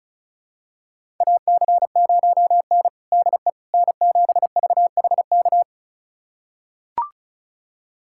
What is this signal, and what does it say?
Morse code, heard as a keyed tone, about 34 wpm, 700 Hz. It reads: AC0N DE N7VHK
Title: USA Callsign DE Callsign 35wpm